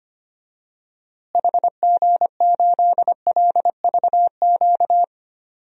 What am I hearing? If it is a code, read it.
Morse code, heard as a keyed tone, about 25 wpm, 700 Hz. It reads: HG8LVQ
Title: DXCC-100 Call Signs - 6 Characters - Repeated 3x - 25wpm